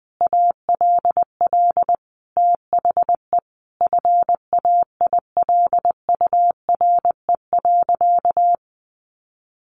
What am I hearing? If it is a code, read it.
Morse code, heard as a keyed tone, about 19 wpm, 700 Hz. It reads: ALL THE FAILURE.
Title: Sentences from Top 900 Words 20wpm